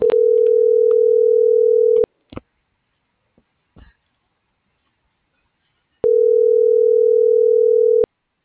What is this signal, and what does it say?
Ringback tone (North America), 440 + 480 Hz.
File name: CENTRAL GOMSA / out alan.escudero